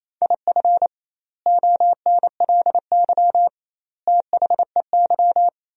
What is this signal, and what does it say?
Morse code, heard as a keyed tone, about 28 wpm, 700 Hz. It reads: IF ONLY THEY
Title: 2x Word-Spacing - Sentences from Top 500 Words - 28wpm